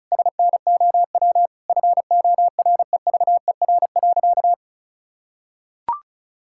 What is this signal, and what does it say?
Morse code, heard as a keyed tone, about 34 wpm, 700 Hz. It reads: SNOW FOREVER.
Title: Sentences from Top 1600 Words 35wpm